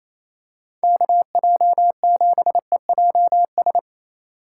Morse code, heard as a keyed tone, about 28 wpm, 700 Hz. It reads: KJ7EJS
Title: USA Call Signs - 6 Characters - 2x Character Spacing - 28wpm